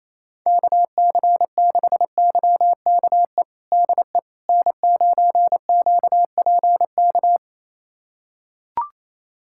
Morse code, heard as a keyed tone, about 28 wpm, 700 Hz. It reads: KC6YKE DE N9QPK
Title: USA Callsign DE Callsign 28wpm